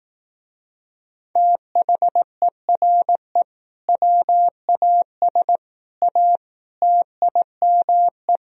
Morse code, heard as a keyed tone, about 18 wpm, 690 Hz. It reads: THERE WAS A TIME
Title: Speed-Racing - Sets of 5 Words 15wpm